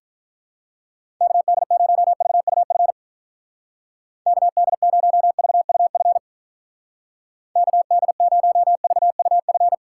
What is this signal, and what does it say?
Morse code, heard as a keyed tone, about 46 wpm, 700 Hz. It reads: KD0VUF KD0VUF KD0VUF